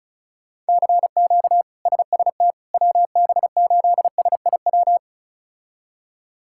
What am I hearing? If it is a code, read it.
Morse code, heard as a keyed tone, about 36 wpm, 700 Hz. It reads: CQ SST WB8SIW
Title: K1USN SST QSO Practice - 35wpm